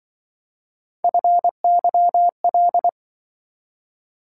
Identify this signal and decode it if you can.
Morse code, heard as a keyed tone, about 25 wpm, 700 Hz. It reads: FYL